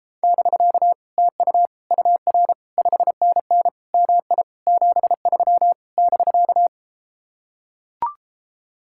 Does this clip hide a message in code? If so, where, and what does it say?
Yes: Morse code from 0.2 to 6.7 s, heard as a keyed tone, about 33 wpm, 700 Hz. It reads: <BK> TU UR 5NN MI 73 <BK>